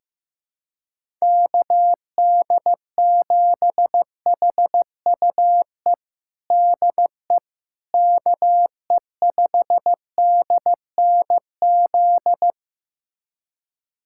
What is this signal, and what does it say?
Morse code, heard as a keyed tone, about 15 wpm, 700 Hz. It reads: KD7HUE DE KE5DNZ
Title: USA Callsign DE Callsign - 6 Characters - 15wpm